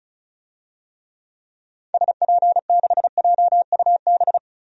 Morse code, heard as a keyed tone, about 36 wpm, 700 Hz. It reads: SP6JUB
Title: DXCC-100 Call Signs - 6 Characters - 35wpm